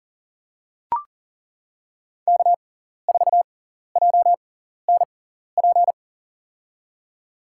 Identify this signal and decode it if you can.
Morse code, heard as a keyed tone, about 40 wpm, spaced as for 14 wpm, 700 Hz. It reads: K4JNP
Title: USA Call Signs - 5 Characters - 6x Character Spacing - 40wpm